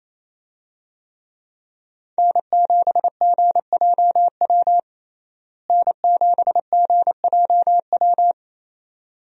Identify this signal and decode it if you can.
Morse code, heard as a keyed tone, about 28 wpm, 700 Hz. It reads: N7GJW N7GJW
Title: USA Call Signs - 5 Characters - Repeated 2x - 28wpm